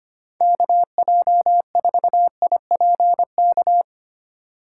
Morse code, heard as a keyed tone, about 25 wpm, 700 Hz. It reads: KJ4IPK
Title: USA Call Signs - 6 Characters - 4x Character Spacing - 25wpm